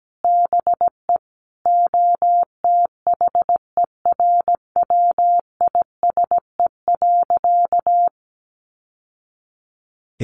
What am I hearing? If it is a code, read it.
Morse code, heard as a keyed tone, about 17 wpm, 700 Hz. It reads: BE OTHERWISE.
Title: Sentences from Top 1200 Words 17wpm